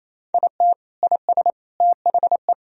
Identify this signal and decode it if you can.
Morse code, heard as a keyed tone, about 28 wpm, 700 Hz. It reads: IT IS THE
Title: Sentences from Top 1000 Words - Review Part 3 - 28wpm